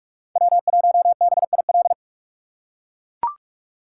Morse code, heard as a keyed tone, about 44 wpm, 700 Hz. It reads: W1BIL